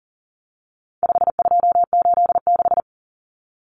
Morse code, heard as a keyed tone, about 40 wpm, 700 Hz. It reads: 5286
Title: Numbers - 4-Digit 40wpm